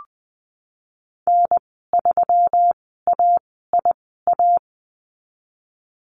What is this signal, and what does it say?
Morse code, heard as a keyed tone, about 19 wpm, spaced as for 14 wpm, 700 Hz. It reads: N3AIA
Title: 122 Lesson 20g - Number 3 - Call Signs 20wpm